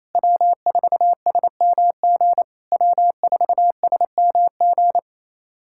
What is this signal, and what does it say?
Morse code, heard as a keyed tone, about 29 wpm, 700 Hz. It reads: W4SMG W4SMG
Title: SOTA QSO Practice 28wpm